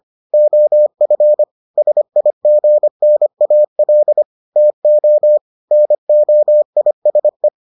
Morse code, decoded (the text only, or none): OF SIGNAL TO NOISE